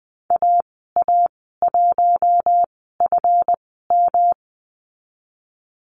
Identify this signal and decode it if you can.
Morse code, heard as a keyed tone, about 20 wpm, spaced as for 14 wpm, 700 Hz. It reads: AA1FM
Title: 154 Lesson 25g - Letter p - Call Signs 20wpm